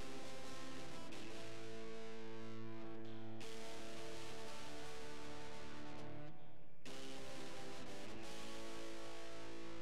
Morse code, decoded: none